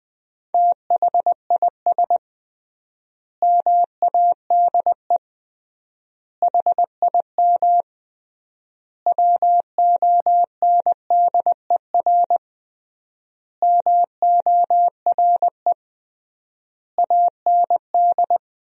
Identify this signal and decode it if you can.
Morse code, heard as a keyed tone, about 20 wpm, 700 Hz. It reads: THIS MADE HIM WONDER MORE AND
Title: 3x Word-Spacing - Sentences from Top 500 Words - 20wpm